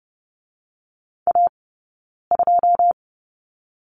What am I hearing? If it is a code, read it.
Morse code, heard as a keyed tone, about 30 wpm, 700 Hz. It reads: A 2
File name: 092 Lesson 16d - Letter c - Two Characters 30wpm